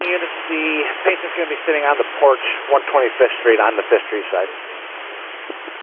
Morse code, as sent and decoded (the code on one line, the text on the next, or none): none